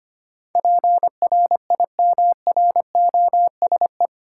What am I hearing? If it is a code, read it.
Morse code, heard as a keyed tone, about 25 wpm, 700 Hz. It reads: PRIMROSE
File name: practice / 25wpm / Difficult Words with Hints - 25wpm